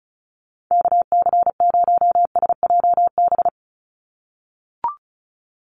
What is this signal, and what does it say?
Morse code, heard as a keyed tone, about 34 wpm, 700 Hz. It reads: KC0SJB
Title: USA Call Signs 35wpm